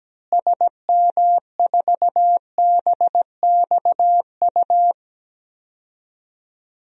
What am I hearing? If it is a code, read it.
Morse code, heard as a keyed tone, about 17 wpm, 700 Hz. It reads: SM4BXU
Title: DXCC-100 Call Signs - 6 Characters - Repeated 3x - 17wpm